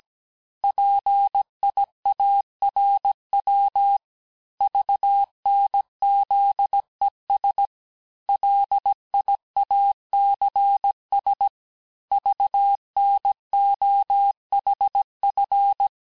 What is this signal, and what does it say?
Morse code, heard as a keyed tone, about 17 wpm, 800 Hz. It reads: PIARW VNZES LIACS VNOHF
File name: corrige-cours-1581-texte-chiffre-vitesse-16-mpm